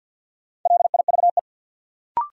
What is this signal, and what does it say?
Morse code, heard as a keyed tone, about 49 wpm, 700 Hz. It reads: LIVE
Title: Sentences from Top 200 Words - New Word Drill - 50wpm